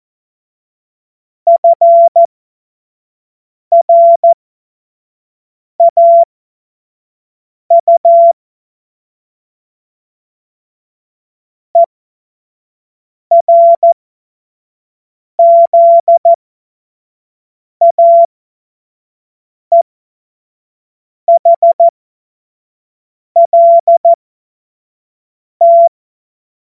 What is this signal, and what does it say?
Morse code, heard as a keyed tone, about 14 wpm, spaced as for 5 wpm, 680 Hz. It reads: FRAU ERZAEHLT
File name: Morsekurs / Witze / Witz 03. Januar 2023 25 F70